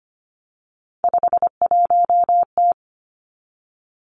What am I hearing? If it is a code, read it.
Morse code, heard as a keyed tone, about 25 wpm, 700 Hz. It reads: 51T
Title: Three Challenging Letter-Number Combinations 25wpm